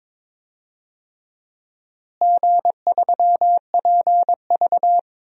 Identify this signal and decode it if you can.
Morse code, heard as a keyed tone, about 22 wpm, 700 Hz. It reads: G3PV
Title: DXCC-100 Call Signs - 4 Characters - Repeated 3x - 22wpm